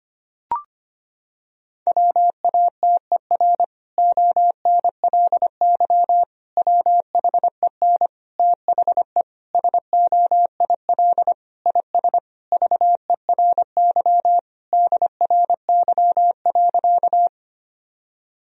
Morse code, heard as a keyed tone, about 25 wpm, 700 Hz. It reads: WATER ONLY WHEN THE SOIL IS VERY DRY.
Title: Sentences from Top 1000 Words - Review Part 4 - 25wpm